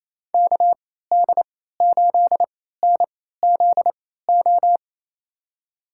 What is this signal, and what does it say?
Morse code, heard as a keyed tone, about 28 wpm, spaced as for 16 wpm, 700 Hz. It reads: KD8NZO
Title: USA Call Signs - 6 Characters - 3x Character Spacing - 28wpm